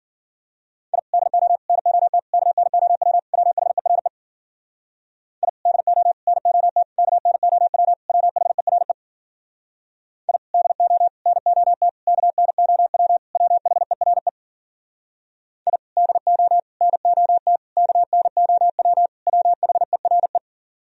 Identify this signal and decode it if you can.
Morse code, heard as a keyed tone, about 52 wpm, 700 Hz. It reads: I DO NOT KNOW WHERE I DO NOT KNOW WHERE I DO NOT KNOW WHERE I DO NOT KNOW WHERE